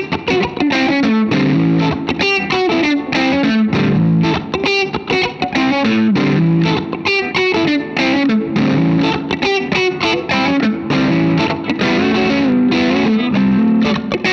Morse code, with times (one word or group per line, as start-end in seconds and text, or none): none